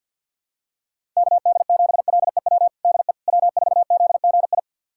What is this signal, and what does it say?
Morse code, heard as a keyed tone, about 52 wpm, 700 Hz. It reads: KD7LEW DE W4ZGI
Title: USA Callsign DE Callsign 50wpm